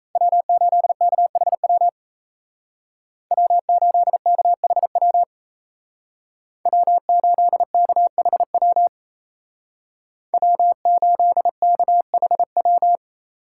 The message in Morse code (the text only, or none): W8KHW W8KHW W8KHW W8KHW